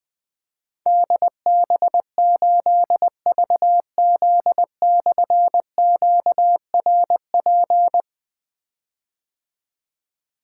Morse code, heard as a keyed tone, about 20 wpm, 700 Hz. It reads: DB8VZ/QRP
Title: Difficult Call Signs 20wpm